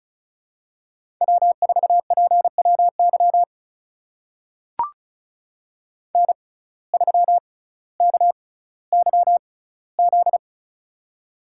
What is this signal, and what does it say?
Morse code, heard as a keyed tone, about 36 wpm, spaced as for 18 wpm, 700 Hz. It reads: W4PWY N3KYZ
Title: USA Call Signs - 5 Characters - 6x Character Spacing - 35wpm